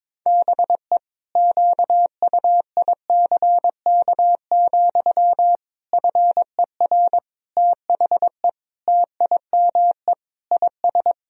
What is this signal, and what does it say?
Morse code, heard as a keyed tone, about 22 wpm, 700 Hz. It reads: BE QUICK, FER THE TIME IS
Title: Sentences from Top 500 Words and Abbreviations 22wpm